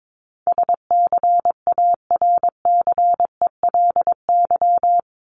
Morse code, heard as a keyed tone, about 22 wpm, 700 Hz. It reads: SCARCELY
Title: LY Suffix - Top 100 Words with Base Words 22wpm